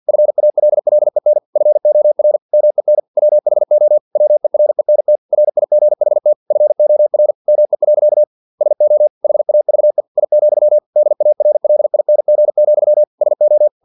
Morse code, heard as a keyed tone, about 50 wpm, 600 Hz. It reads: FALLEN FOR MEN WHO WERENT RIGHT FOR ME. SO HAVE I, DARLING, SO
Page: Never 49wpm